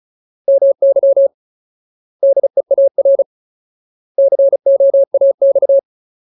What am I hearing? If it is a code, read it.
Morse code, heard as a keyed tone, about 34 wpm, 550 Hz. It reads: MY DEAR COAX
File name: Ultimate Word List / 35WPM Ultimate